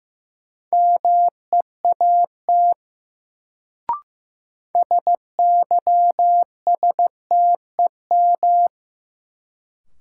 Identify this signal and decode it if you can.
Morse code, heard as a keyed tone, about 15 wpm, 700 Hz. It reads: MEAT SYSTEM
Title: Sentences from Top 1000 Words - Review Part 3 - New Word Drill - 15wpm